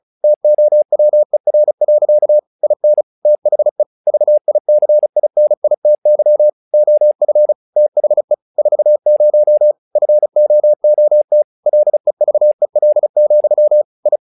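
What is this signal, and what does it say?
Morse code, heard as a keyed tone, about 36 wpm, 600 Hz. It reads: TOWER. IN THE VICINITY OF THE 40 FOOT LEVEL, I
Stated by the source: Morse practice text "Tower Trouble" sent at 35 WPM, from the SKCC - Straight Key Century Club learning center audio files.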